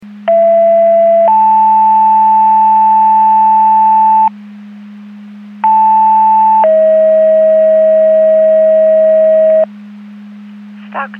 Two-tone page, heard as a one-second tone, then a three-second tone, about 670 then 880 Hz, about 880 then 640 Hz. It is heard from 0.3 to 9.6 s.